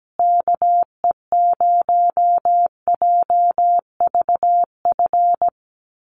Morse code, heard as a keyed tone, about 17 wpm, 700 Hz. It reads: KE0JVF
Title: USA Call Signs Repeated 2x 17wpm